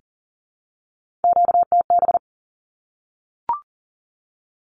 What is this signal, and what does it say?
Morse code, heard as a keyed tone, about 40 wpm, 700 Hz. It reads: QTB
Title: Q-Codes Repeated 3x 40wpm